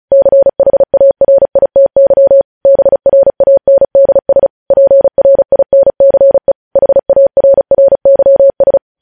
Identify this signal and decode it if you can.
Morse code, heard as a keyed tone, about 36 wpm, 550 Hz. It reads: CHARITY BRANDS PRINCE HARRYS
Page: News headlines in Morse code at 35 wpm